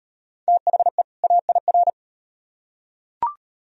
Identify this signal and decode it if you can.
Morse code, heard as a keyed tone, about 35 wpm, 700 Hz. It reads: THE AIR